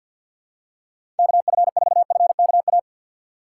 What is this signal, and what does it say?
Morse code, heard as a keyed tone, about 53 wpm, 700 Hz. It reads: KF4FKA
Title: USA Call Signs - 2x Character Spacing - 50wpm